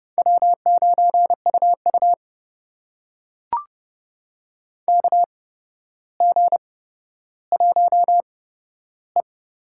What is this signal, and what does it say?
Morse code, heard as a keyed tone, about 30 wpm, spaced as for 13 wpm, 700 Hz. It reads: W9UU KG1E
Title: USA Call Signs - 4 Characters - 8x Character Spacing - 30wpm